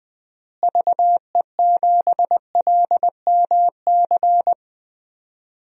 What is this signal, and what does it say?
Morse code, heard as a keyed tone, about 20 wpm, 700 Hz. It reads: VE7LMC
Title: DXCC-100 Call Signs - 6 Characters - Repeated 3x - 20wpm